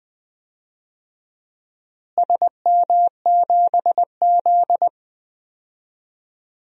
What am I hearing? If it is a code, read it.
Morse code, heard as a keyed tone, about 20 wpm, 700 Hz. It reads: SM7Z
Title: DXCC-100 Call Signs - 4 Characters - Repeated 3x - 20wpm